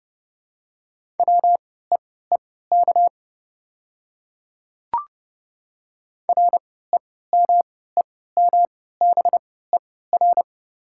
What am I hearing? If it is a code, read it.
Morse code, heard as a keyed tone, about 31 wpm, spaced as for 17 wpm, 690 Hz. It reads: WEEK REMEMBER
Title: 3x Character Spacing - Top 500 Words - 30wpm